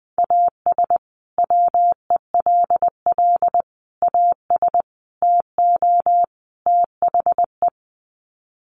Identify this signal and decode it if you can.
Morse code, heard as a keyed tone, about 20 wpm, 700 Hz. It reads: AS WELL AS TO THE
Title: Sets of 5 Words Encore2 20wpm